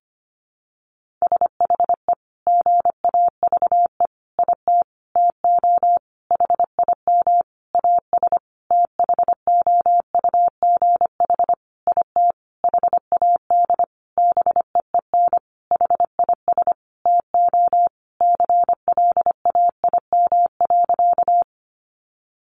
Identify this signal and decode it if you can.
Morse code, heard as a keyed tone, about 25 wpm, 700 Hz. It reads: SHE GAVE IT TO HIM AS THOUGH IT HAD BEEN HIS TO CLAIM.
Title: Sentences from Top 1500 Words 25wpm